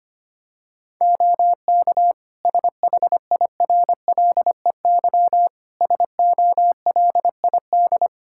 Morse code, heard as a keyed tone, about 25 wpm, 700 Hz. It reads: OK SHIRLEY SOLID
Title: Practice QSO - Exchange 3 - More Info - 25wpm